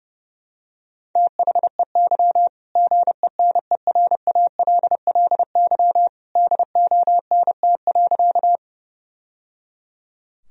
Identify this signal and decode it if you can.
Morse code, heard as a keyed tone, about 29 wpm, 700 Hz. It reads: THEY GENERALLY DONT.
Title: Sentences from Top 1000 Words - Review Part 4 - 30wpm